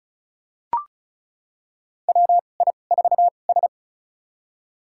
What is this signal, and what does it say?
Morse code, heard as a keyed tone, about 34 wpm, spaced as for 25 wpm, 690 Hz. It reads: WI4S
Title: USA Call Signs - 4 Characters - 2x Character Spacing - 35wpm